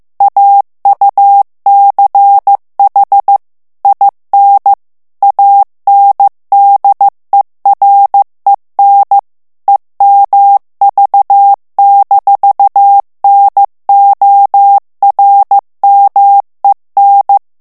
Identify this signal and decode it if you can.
Morse code, heard as a keyed tone, about 15 wpm, 800 Hz. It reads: AUCH IN ANDEREN EMV-NORMEN
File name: morse mp3 / ALLEMAND texte clair 1 vitesse 14 mots minute